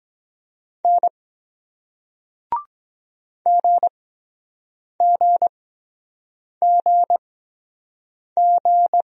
Morse code, heard as a keyed tone, about 22 wpm, spaced as for 7 wpm, 700 Hz. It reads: N GGGG